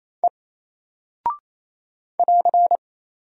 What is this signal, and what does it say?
Morse code, heard as a keyed tone, about 28 wpm, 700 Hz. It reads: E <AR>
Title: Single Character (Letter:Number:Punctuation:Prosign) Alt 28wpm